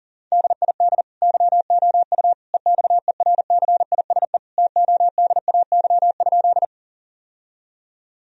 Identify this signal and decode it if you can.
Morse code, heard as a keyed tone, about 40 wpm, 700 Hz. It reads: DID YOU EXERCISE TODAY?
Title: Sentences from Top 1000 Words - Review Part 4 - 40wpm